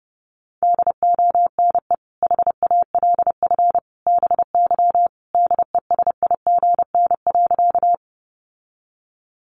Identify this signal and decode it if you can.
Morse code, heard as a keyed tone, about 30 wpm, 700 Hz. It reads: DONE HALF BY DESIGN.
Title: Sentences from Top 800 Words 30wpm